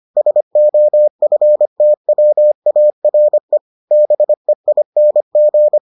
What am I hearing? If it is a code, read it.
Morse code, heard as a keyed tone, about 25 wpm, 600 Hz. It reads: SOFTWARE BEING